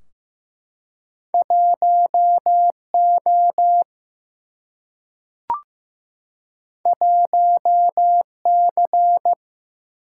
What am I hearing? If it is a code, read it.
Morse code, heard as a keyed tone, about 15 wpm, 700 Hz. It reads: 1 O 1 C
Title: Single-Letter Single-Number 15wpm